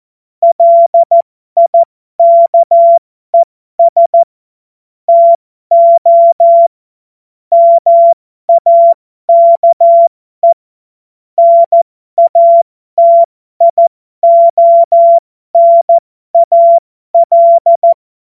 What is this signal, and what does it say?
Morse code, heard as a keyed tone, about 14 wpm, spaced as for 12 wpm, 680 Hz. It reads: LIKES TO MAKE NATIONAL
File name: Morsekurs / Witze